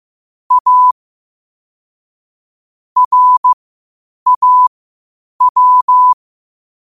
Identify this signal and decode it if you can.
Morse code, heard as a keyed tone, about 15 wpm, spaced as for 8 wpm, 1000 Hz. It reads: A RAW
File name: qotdfiles / 8wpm